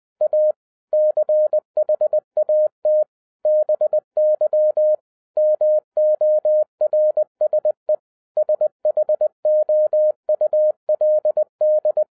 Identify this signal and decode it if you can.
Morse code, heard as a keyed tone, about 20 wpm, 610 Hz. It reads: A CHAT BY MORSE SHOULD
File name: 2012-01-mp3 / n16-THROUGH EIGHT MONTHS-20wpm